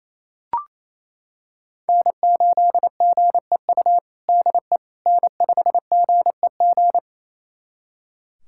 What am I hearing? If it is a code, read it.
Morse code, heard as a keyed tone, about 28 wpm, 700 Hz. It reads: N8GEU DE N5GEG
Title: USA Callsign DE Callsign - 5 Characters - 28wpm